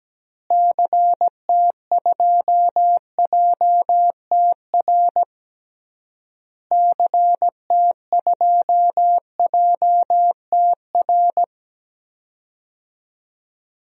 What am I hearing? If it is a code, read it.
Morse code, heard as a keyed tone, about 17 wpm, 700 Hz. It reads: CT2JTR CT2JTR
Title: DXCC-100 Call Signs - 6 Characters - Repeated 3x - 17wpm